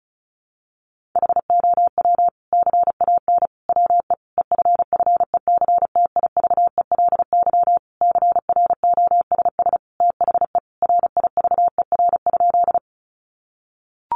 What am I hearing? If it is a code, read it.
Morse code, heard as a keyed tone, about 34 wpm, 700 Hz. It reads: HOW CAN WE EFFECTIVELY CROSS THE RIVER?
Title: Sentences from Top 1400 Words 35wpm